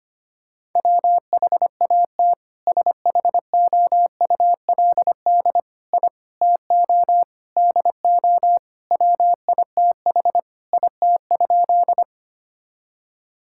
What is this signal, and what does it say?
Morse code, heard as a keyed tone, about 25 wpm, 700 Hz. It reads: WHAT SHOULD I TO DO WITH IT?
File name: practice / 25wpm / Sentences with 4-Word Phrases - 25wpm